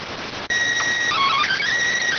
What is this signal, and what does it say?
SSTV (slow-scan television), PD 290.